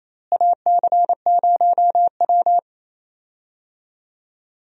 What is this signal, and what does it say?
Morse code, heard as a keyed tone, about 28 wpm, 700 Hz. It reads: AC0W
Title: USA Call Signs - 4 Characters - Repeated 3x - 28wpm